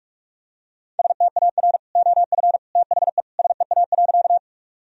Morse code, heard as a keyed tone, about 46 wpm, 700 Hz. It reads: STAR OF THE SEA.
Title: Sentences from Top 1000 Words - Review Part 2 - 45wpm